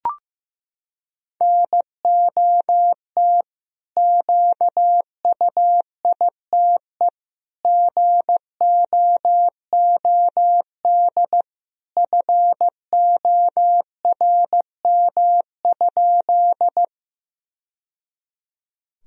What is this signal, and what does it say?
Morse code, heard as a keyed tone, about 15 wpm, 700 Hz. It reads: NOT QUITE GOOD FORM?
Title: Sentences from Top 1000 Words - Review Part 1 - 15wpm